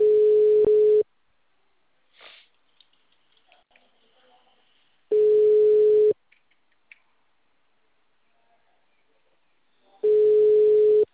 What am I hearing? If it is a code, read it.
Ringback tone (Europe, 425 Hz).
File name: out- -2018